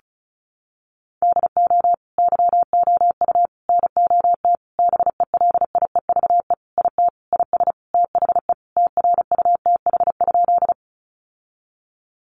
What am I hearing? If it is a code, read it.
Morse code, heard as a keyed tone, about 34 wpm, 700 Hz. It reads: DO YOU NOT BELIEVE IT IS THE TRUTH?
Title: Sentences from Top 1000 Words 35wpm